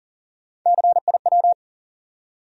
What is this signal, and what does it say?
Morse code, heard as a keyed tone, about 40 wpm, 700 Hz. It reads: CIW
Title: USA Call Sign Postfixes - 3 Letters 40wpm